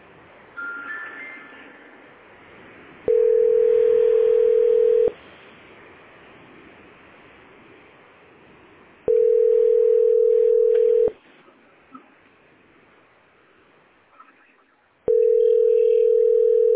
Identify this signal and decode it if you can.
Ringback tone (North America), 440 + 480 Hz.